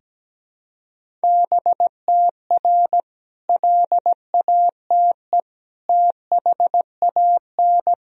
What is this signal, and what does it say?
Morse code, heard as a keyed tone, about 17 wpm, 700 Hz. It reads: BTR LATE THAN
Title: Sentences from Top 400 Words and Abbreviations 17wpm